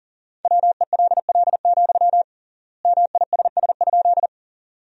Morse code, heard as a keyed tone, about 40 wpm, 700 Hz. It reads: WELL, MISS?
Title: 3x Word-Spacing - Sentences from Top 500 Words - 40wpm